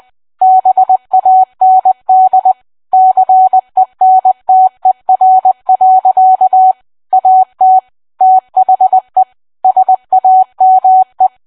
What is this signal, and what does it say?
Morse code, heard as a keyed tone, about 20 wpm, 750 Hz. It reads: BAND CENTER. AT THE SAME